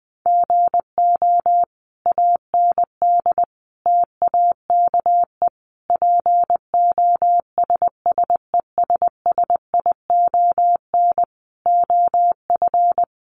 Morse code, heard as a keyed tone, about 20 wpm, 700 Hz. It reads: GO AND TAKE POSSESSION OF
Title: Sentences from Top 1400 Words 20wpm